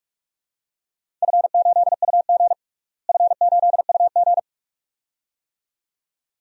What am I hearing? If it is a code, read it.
Morse code, heard as a keyed tone, about 46 wpm, 700 Hz. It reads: F8UG F8UG
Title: DXCC-100 Call Signs - 4 Characters - Repeated 2x - 45wpm